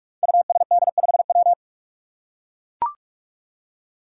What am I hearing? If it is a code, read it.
Morse code, heard as a keyed tone, about 44 wpm, 700 Hz. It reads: USD5W
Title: Five Challenging Letter-Number Combinations 45wpm